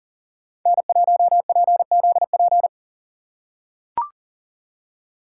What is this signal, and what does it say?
Morse code, heard as a keyed tone, about 40 wpm, 700 Hz. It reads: N1PZP
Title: USA Call Signs - 5 Characters - 3x Character Spacing - 40wpm